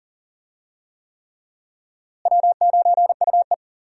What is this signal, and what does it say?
Morse code, heard as a keyed tone, about 40 wpm, 700 Hz. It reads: W9UE
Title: USA Call Signs - 4 Characters - Repeated 3x - 40wpm